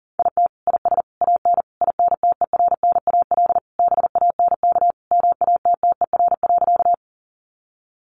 Morse code, heard as a keyed tone, about 40 wpm, 700 Hz. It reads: IT IS AN INTERNAL BANK MATTER.
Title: Sentences from Top 1400 Words 40wpm